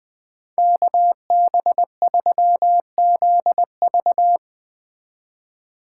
Morse code, heard as a keyed tone, about 20 wpm, 700 Hz. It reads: KB3ZV
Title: Speed Racing - USA 2x2 Call Signs 17wpm